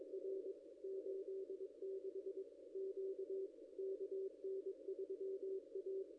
Morse code, heard as a keyed tone, about 22 wpm, 380 Hz. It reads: R 86 Q KN3A
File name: contests / ss-cw-k5tr-op-n5rz -0500